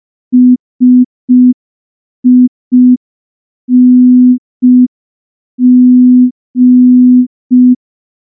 Morse code, heard as a keyed tone, about 5 wpm, 250 Hz. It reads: SING
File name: morse / morse-training-05WPM